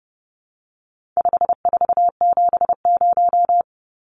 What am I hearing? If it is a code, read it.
Morse code, heard as a keyed tone, about 30 wpm, 700 Hz. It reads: 5470
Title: Numbers - 4-Digit 30wpm